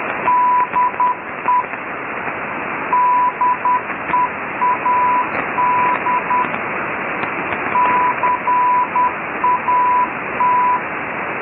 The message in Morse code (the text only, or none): DE DEAD CAT